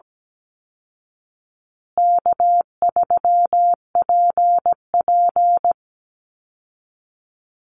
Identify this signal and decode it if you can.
Morse code, heard as a keyed tone, about 17 wpm, 700 Hz. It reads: K3PP
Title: Easy Call Signs Repeated 3x - 17wpm